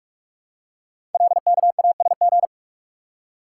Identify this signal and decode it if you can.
Morse code, heard as a keyed tone, about 44 wpm, 700 Hz. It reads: LKASG
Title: Code Groups - 5 Letters - 45wpm